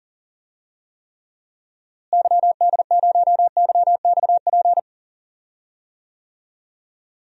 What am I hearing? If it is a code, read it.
Morse code, heard as a keyed tone, about 39 wpm, 700 Hz. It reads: YD0YXP
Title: DXCC-100 Call Signs - 6 Characters - Repeated 2x - 40wpm